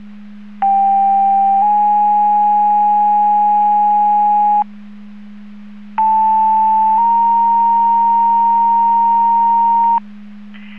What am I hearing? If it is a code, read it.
Two-tone page, heard as a one-second tone, then a three-second tone, about 790 then 830 Hz, about 880 then 930 Hz. It is heard from 0.6 to 10.0 s.